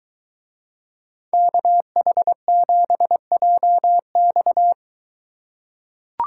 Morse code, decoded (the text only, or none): KH7JX